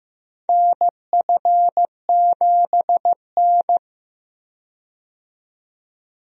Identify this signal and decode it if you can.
Morse code, heard as a keyed tone, about 15 wpm, 700 Hz. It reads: NF7N
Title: USA Call Signs - 4 Characters - 15wpm